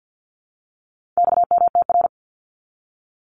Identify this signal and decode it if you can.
Morse code, heard as a keyed tone, about 50 wpm, 700 Hz. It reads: XMTR
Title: Abbreviations 50wpm